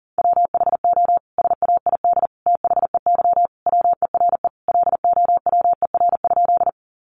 Morse code, heard as a keyed tone, about 40 wpm, 700 Hz. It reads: WHO SAID THEY WERE LOWER?
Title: Sentences from Top 700 Words 40wpm